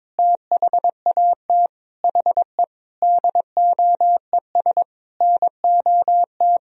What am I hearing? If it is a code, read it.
Morse code, heard as a keyed tone, about 22 wpm, 700 Hz. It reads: THAT HE DOES NOT
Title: Sentences from Top 1000 Words - Review Part 2 - 22wpm